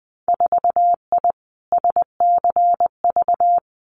Morse code, heard as a keyed tone, about 20 wpm, 700 Hz. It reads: 4I SCV
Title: Winter Field Day Exchanges Encore 20wpm